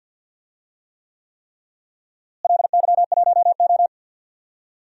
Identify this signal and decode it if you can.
Morse code, heard as a keyed tone, about 50 wpm, 700 Hz. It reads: LY1O
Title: DXCC-100 Call Signs - 4 Characters - Repeated 3x - 50wpm